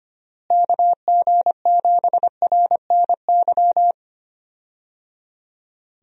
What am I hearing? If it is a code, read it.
Morse code, heard as a keyed tone, about 25 wpm, 700 Hz. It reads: KG7RNY
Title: USA Call Signs - 6 Characters - Repeated 3x - 25wpm